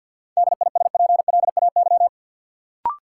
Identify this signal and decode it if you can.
Morse code, heard as a keyed tone, about 53 wpm, 700 Hz. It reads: DISPLAY
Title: Sentences from Top 1200 Words - New Word Drill - 50wpm